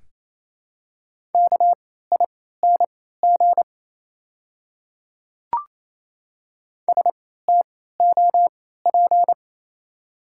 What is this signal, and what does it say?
Morse code, heard as a keyed tone, about 28 wpm, spaced as for 16 wpm, 700 Hz. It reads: KING STOP
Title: 3x Character Spacing - Top 500 Words - 28wpm